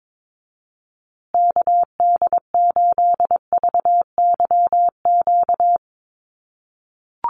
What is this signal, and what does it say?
Morse code, heard as a keyed tone, about 22 wpm, 700 Hz. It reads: KD8VYQ
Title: USA Call Signs Repeated 2x 22wpm